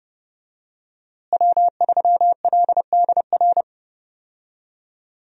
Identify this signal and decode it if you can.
Morse code, heard as a keyed tone, about 30 wpm, 700 Hz. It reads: W3LDR
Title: USA Call Signs - 5 Characters - Repeated 3x - 30wpm